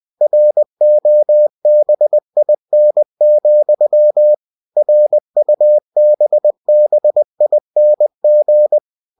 Morse code, heard as a keyed tone, about 20 wpm, 600 Hz. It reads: ROBIN, RUBBING
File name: cw-books / winnie-the-pooh / winnie-20wpm-page-0088